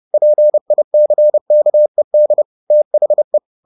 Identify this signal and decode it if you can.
Morse code, heard as a keyed tone, about 30 wpm, 600 Hz. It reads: PICKED THE